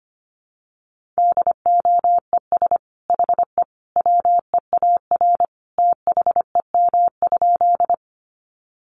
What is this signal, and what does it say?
Morse code, heard as a keyed tone, about 25 wpm, 700 Hz. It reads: DOES HE WEAR THEM?
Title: Sentences from Top 1500 Words 25wpm